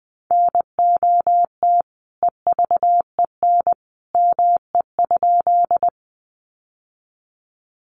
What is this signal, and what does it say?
Morse code, heard as a keyed tone, about 20 wpm, 700 Hz. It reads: NOT EVEN ME?
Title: Sentences from Top 600 Words 20wpm